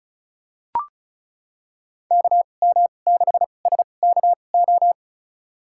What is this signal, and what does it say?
Morse code, heard as a keyed tone, about 36 wpm, spaced as for 25 wpm, 700 Hz. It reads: KM6SKO
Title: USA Call Signs - 2x Character Spacing - 35wpm